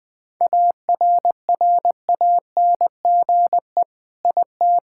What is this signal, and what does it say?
Morse code, heard as a keyed tone, about 20 wpm, 700 Hz. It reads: ARRANGE IT
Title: Sentences with 5-Word Phrases - 20wpm